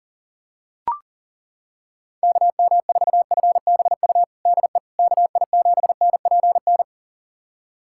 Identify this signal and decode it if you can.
Morse code, heard as a keyed tone, about 40 wpm, 700 Hz. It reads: KM4FBU DE KI7NPN
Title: USA Callsign DE Callsign 40wpm